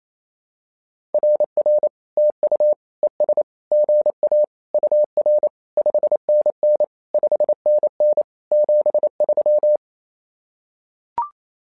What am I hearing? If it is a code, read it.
Morse code, heard as a keyed tone, about 29 wpm, 600 Hz. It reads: RR TU ES GA UR 5NN 5NN 73
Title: SOTA QSO Practice 28wpm